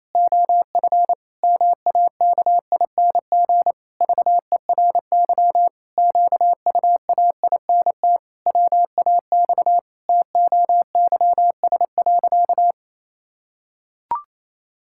Morse code, heard as a keyed tone, about 28 wpm, 700 Hz. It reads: OF MAKING VERY QUAINT WAX TOYS.
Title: Pangram Challenge v2 - 28wpm